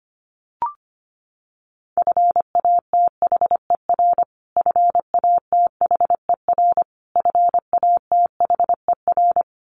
Morse code, heard as a keyed tone, about 25 wpm, 700 Hz. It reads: FATHER FATHER FATHER
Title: Top 500 Words Repeated 25wpm